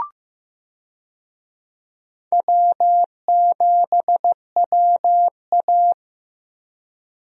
Morse code, heard as a keyed tone, about 16 wpm, 700 Hz. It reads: W7WA
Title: USA Call Signs - 4 Characters - Repeated 3x - 15wpm